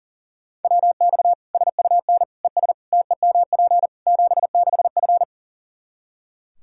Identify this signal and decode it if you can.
Morse code, heard as a keyed tone, about 40 wpm, 700 Hz. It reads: WX SUN ES TEMP 76F
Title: K1USN SST QSO Practice 40wpm